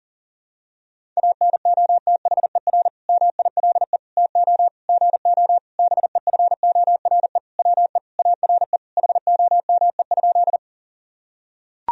Morse code, heard as a keyed tone, about 40 wpm, 700 Hz. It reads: ANOTHER MILE TO GO BEFORE WE ARE HOME?
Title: Sentences from Top 1000 Words - Review Part 1 - 40wpm